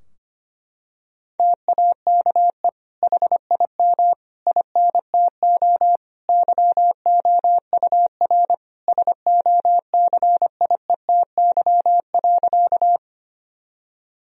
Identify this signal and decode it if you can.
Morse code, heard as a keyed tone, about 25 wpm, 700 Hz. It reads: TAKE HIM INTO YOUR SOCIETY.
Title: Sentences from Top 1000 Words - Review Part 4 - 25wpm